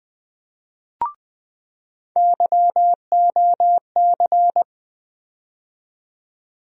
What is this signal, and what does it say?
Morse code, heard as a keyed tone, about 20 wpm, 700 Hz. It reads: YOC
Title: USA Call Sign Postfixes - 3 Letters 20wpm